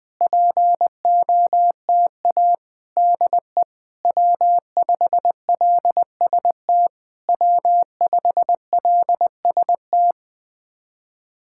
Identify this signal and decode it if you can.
Morse code, heard as a keyed tone, about 20 wpm, 700 Hz. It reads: POTA DE W5LST W5LST
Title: POTA QSO Practice - 20wpm